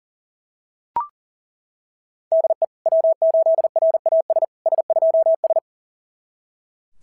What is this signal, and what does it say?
Morse code, heard as a keyed tone, about 40 wpm, 690 Hz. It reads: DE W8RAS S2S
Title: SOTA QSO Practice 40wpm